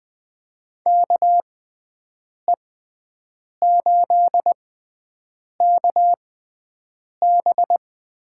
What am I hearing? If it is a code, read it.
Morse code, heard as a keyed tone, about 21 wpm, spaced as for 7 wpm, 700 Hz. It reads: KE8KB
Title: USA Call Signs - 6x Character Spacing - 20wpm